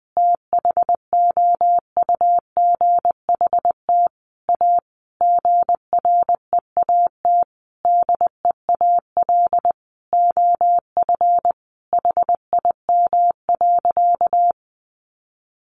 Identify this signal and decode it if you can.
Morse code, heard as a keyed tone, about 20 wpm, 700 Hz. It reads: THOUGHT A GREAT DEAL OF HIM.
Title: Sentences from Top 800 Words 20wpm